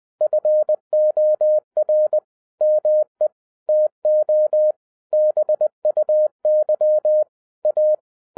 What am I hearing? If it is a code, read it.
Morse code, heard as a keyed tone, about 20 wpm, 610 Hz. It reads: FOR ME TO BUY A
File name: 2012-01-mp3 / n05-CAME BACK TO HAM RADIO AFTER 30 YEARS OF INTERVAL-20wpm